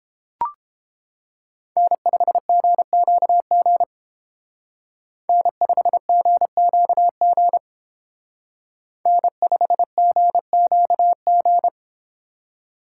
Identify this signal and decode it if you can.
Morse code, heard as a keyed tone, about 29 wpm, 700 Hz. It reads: N5GQG N5GQG N5GQG